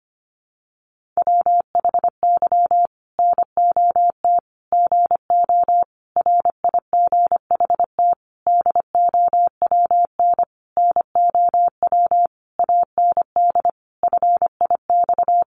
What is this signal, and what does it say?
Morse code, heard as a keyed tone, about 25 wpm, 700 Hz. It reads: WHY NOT GO RIGHT DOWN NOW AND FIX
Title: Sentences from Top 1400 Words 25wpm